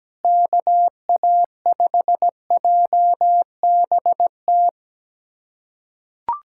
Morse code, heard as a keyed tone, about 17 wpm, 700 Hz. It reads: KA5JBT
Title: USA Callsign DE Callsign - 6 Characters - 17wpm